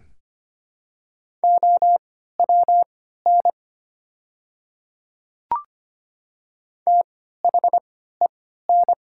Morse code, heard as a keyed tone, about 25 wpm, spaced as for 14 wpm, 700 Hz. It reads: OWN THEN
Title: 3x Character Spacing - Top 300 Words - 25wpm